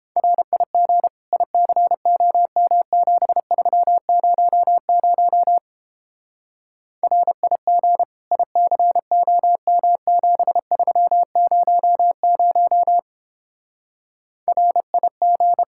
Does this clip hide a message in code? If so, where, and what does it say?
Yes: Morse code from 0.2 to 15.6 s, heard as a keyed tone, about 31 wpm, 700 Hz. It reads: RIG ICOM 7300 RIG ICOM 7300 RIG